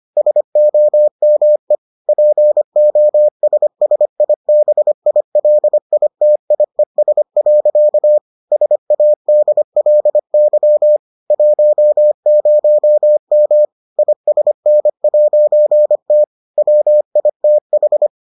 Morse code, heard as a keyed tone, about 25 wpm, 600 Hz. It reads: SOME POSSIBILITIES. SADLY 10M ISN'T WITH